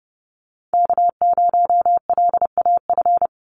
Morse code, heard as a keyed tone, about 30 wpm, 700 Hz. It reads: K0LAF
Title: NA SOTA Chaser Callsigns 30wpm